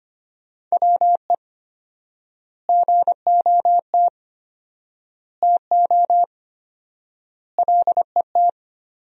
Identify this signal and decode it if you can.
Morse code, heard as a keyed tone, about 25 wpm, 700 Hz. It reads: WE GOT TO LET
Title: 4x Word-Spacing - Sentences from Top 300 Words - 25wpm